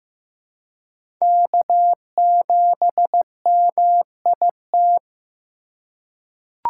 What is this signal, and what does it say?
Morse code, heard as a keyed tone, about 15 wpm, 700 Hz. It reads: K7MIT
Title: USA Call Signs - 5 Characters - 4x Character Spacing - 15wpm